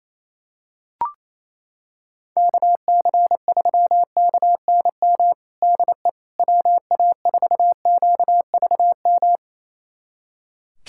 Morse code, heard as a keyed tone, about 28 wpm, 700 Hz. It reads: KC3KNM DE WA4QVM
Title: USA Callsign DE Callsign - 6 Characters - 28wpm